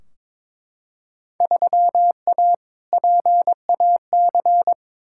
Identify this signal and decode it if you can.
Morse code, heard as a keyed tone, about 21 wpm, 700 Hz. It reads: 3A PAC
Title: Field Day Exchanges - 22wpm